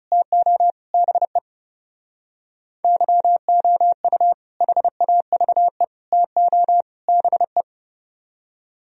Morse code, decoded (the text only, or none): TO BE YOU HAVE TO BE